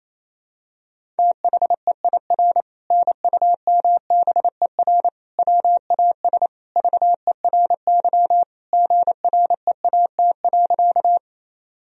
Morse code, heard as a keyed tone, about 28 wpm, 700 Hz. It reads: THEIR NUMBER WAS VERY GREAT.
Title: Sentences from Top 1000 Words - Review Part 1 - 28wpm